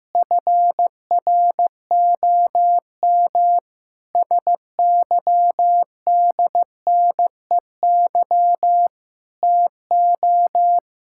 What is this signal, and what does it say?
Morse code, heard as a keyed tone, about 16 wpm, 700 Hz. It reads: FROM SYDNEY TO
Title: Pangram Challenge v2 - 15wpm